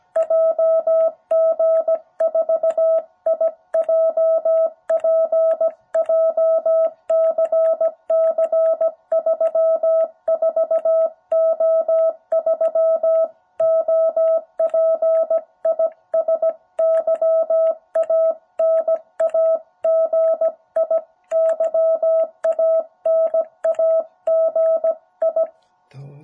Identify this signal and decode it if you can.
Morse code, heard as a keyed tone, about 17 wpm, 660 Hz. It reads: JG4IJP JCC 34O3 OP IS YANAGI YANAGI
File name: jh8cbh / 2021cw